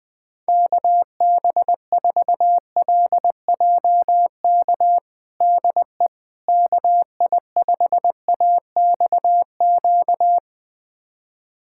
Morse code, heard as a keyed tone, about 20 wpm, 700 Hz. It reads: KB4LJK DE KI5AXQ
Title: USA Callsign DE Callsign - 6 Characters - 20wpm